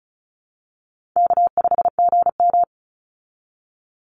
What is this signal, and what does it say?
Morse code, heard as a keyed tone, about 34 wpm, 700 Hz. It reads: K5GM
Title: Easy Call Signs Repeated 2x - 35wpm